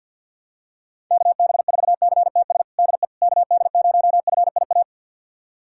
Morse code, heard as a keyed tone, about 50 wpm, 700 Hz. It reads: KB4CTS DE KD0FIA
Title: USA Callsign DE Callsign 50wpm